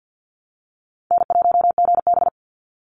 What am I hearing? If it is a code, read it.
Morse code, heard as a keyed tone, about 49 wpm, 700 Hz. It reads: N1GB
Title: NA SOTA Chaser Callsigns 50wpm